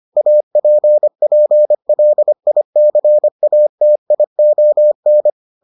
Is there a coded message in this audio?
Yes: Morse code from 0.2 to 5.3 s, heard as a keyed tone, about 25 wpm, 600 Hz. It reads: APPLICATION